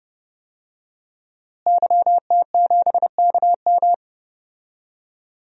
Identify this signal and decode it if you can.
Morse code, heard as a keyed tone, about 30 wpm, 700 Hz. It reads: YT7KM
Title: DXCC-100 Call Signs - 5 Characters - Repeated 2x - 30wpm